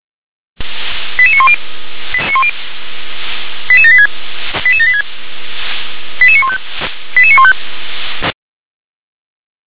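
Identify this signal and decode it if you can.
Five-tone selective call (ZVEI), heard as quick runs of five tones, tones of about 80 ms. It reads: 88910 88786 88786 88915 88915
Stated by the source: ZVEI five-tone selective call received by SDR.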